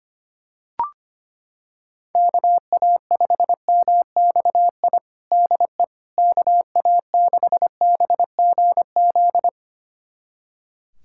Morse code, heard as a keyed tone, about 25 wpm, 700 Hz. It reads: KA5MXI DE KA6BGZ
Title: USA Callsign DE Callsign - 6 Characters - 25wpm